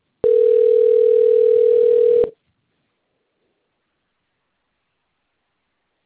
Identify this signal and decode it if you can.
Ringback tone (North America), 440 + 480 Hz.
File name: rg-1600-unknown- - - .16078